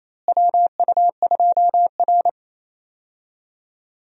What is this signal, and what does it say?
Morse code, heard as a keyed tone, about 28 wpm, 700 Hz. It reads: WU2R
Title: USA Call Signs - 4 Characters - Repeated 2x - 28wpm